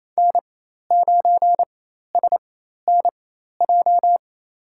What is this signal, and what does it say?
Morse code, heard as a keyed tone, about 27 wpm, spaced as for 13 wpm, 700 Hz. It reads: N9SNJ
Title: USA Call Signs - 5 Characters - 4x Character Spacing - 28wpm